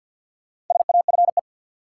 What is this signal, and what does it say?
Morse code, heard as a keyed tone, about 47 wpm, 700 Hz. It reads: SAFE
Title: Sentences from Top 900 Words - New Word Drill - 50wpm